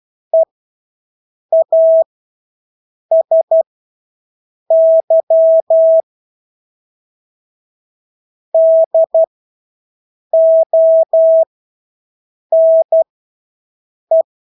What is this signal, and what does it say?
Morse code, heard as a keyed tone, about 12 wpm, spaced as for 6 wpm, 650 Hz. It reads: EASY DONE
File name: Engelske ord 12-6 wpm